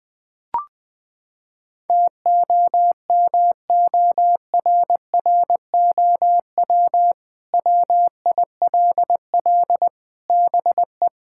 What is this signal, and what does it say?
Morse code, heard as a keyed tone, about 21 wpm, 700 Hz. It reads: TOMORROW WILL BE
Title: Sentences with 2-Word Phrases - 20wpm